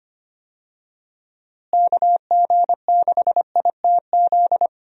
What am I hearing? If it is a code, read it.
Morse code, heard as a keyed tone, about 25 wpm, 700 Hz. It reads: KG6ITZ
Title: USA Call Signs - 6 Characters - Repeated 2x - 25wpm